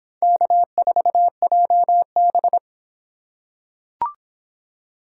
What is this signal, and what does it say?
Morse code, heard as a keyed tone, about 26 wpm, 700 Hz. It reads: K4JB